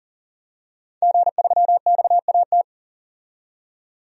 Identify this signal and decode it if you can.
Morse code, heard as a keyed tone, about 38 wpm, 700 Hz. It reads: G3XAT
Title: DXCC-100 Call Signs - 5 Characters - 40wpm